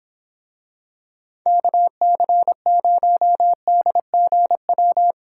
Morse code, heard as a keyed tone, about 26 wpm, 700 Hz. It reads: KC0DGW